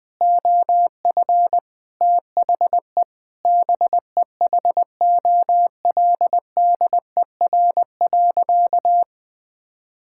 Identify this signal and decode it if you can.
Morse code, heard as a keyed tone, about 20 wpm, 700 Hz. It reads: OF THE BEHOLDER.
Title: Sentences with 8-Word Phrases - Spoken Beginning - 20wpm